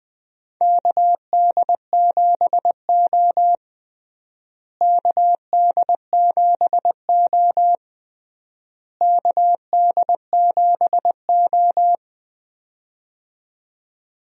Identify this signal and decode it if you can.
Morse code, heard as a keyed tone, about 21 wpm, 700 Hz. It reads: KD7O KD7O KD7O
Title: USA Call Signs - 4 Characters - Repeated 3x - 20wpm